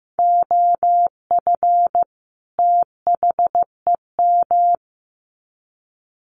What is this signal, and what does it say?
Morse code, heard as a keyed tone, about 15 wpm, 700 Hz. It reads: OF THEM
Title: Sets of 2 Words 15wpm